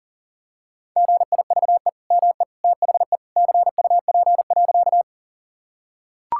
Morse code, heard as a keyed tone, about 40 wpm, 700 Hz. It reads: GIVE ME THE CUP.
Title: Sentences from Top 1600 Words 40wpm